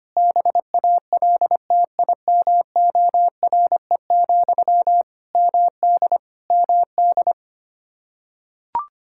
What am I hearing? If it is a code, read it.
Morse code, heard as a keyed tone, about 25 wpm, 700 Hz. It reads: BALTIMORE, MD MD
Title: Practice QSO - Exchange 1 - Beginning - 25wpm